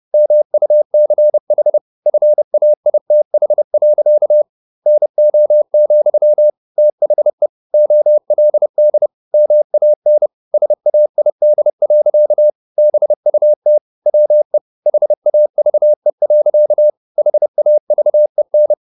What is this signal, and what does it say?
Morse code, heard as a keyed tone, about 30 wpm, 600 Hz. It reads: MUCH FAITH. NO, THE OLD MAN SAID. BUT WE HAVE. HAVEN
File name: the old man and the sea / the old man and the sea 1 30wpm